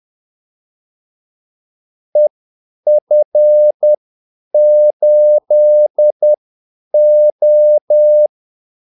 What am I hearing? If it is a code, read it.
Morse code, heard as a keyed tone, about 10 wpm, spaced as for 8 wpm, 600 Hz. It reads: EF8O
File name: Callsigns-10-8wpm